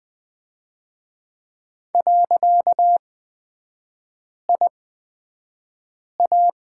Morse code, heard as a keyed tone, about 20 wpm, spaced as for 5 wpm, 700 Hz. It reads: .IA
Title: 255 Lesson 41e - Period - Three Characters 20wpm